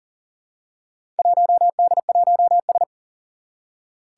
Morse code, heard as a keyed tone, about 38 wpm, 700 Hz. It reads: 1D1S
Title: Four Challenging Letter-Number Combinations 40wpm